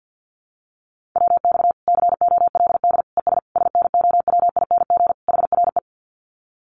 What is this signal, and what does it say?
Morse code, heard as a keyed tone, about 50 wpm, 700 Hz. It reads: WX COLD ES SNOWING HRE
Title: Common QSO Phrases 50wpm